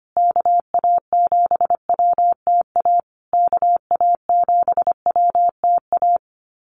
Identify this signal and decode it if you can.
Morse code, heard as a keyed tone, about 25 wpm, 700 Hz. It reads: KA7WTA KA7WTA
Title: USA Call Signs Repeated 3x 25wpm